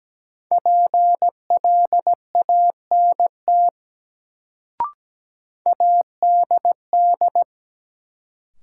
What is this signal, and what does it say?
Morse code, heard as a keyed tone, about 17 wpm, 700 Hz. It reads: PLANT ADD
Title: Sentences from Top 1000 Words - Review Part 1 - New Word Drill - 17wpm